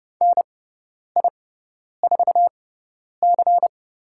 Morse code, heard as a keyed tone, about 30 wpm, spaced as for 10 wpm, 700 Hz. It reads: NI4C
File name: Lesson 17g - Letter u - Call Signs 30wpm